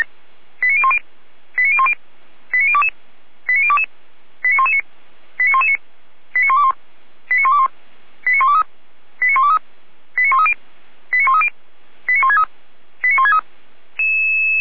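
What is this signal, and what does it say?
Five-tone selective call (ZVEI), heard as quick runs of five tones, tones of about 70 ms. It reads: 89010 89010 89022 89022 89119 89119 89121 89121 89123 89123 89130 89130 89163 89163
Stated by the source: ZVEI five-tone selective call received by SDR.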